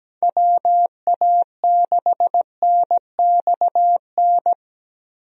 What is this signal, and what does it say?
Morse code, heard as a keyed tone, about 17 wpm, 700 Hz. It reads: WA6NXN
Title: USA Call Signs - 6 Characters - Repeated 2x - 17wpm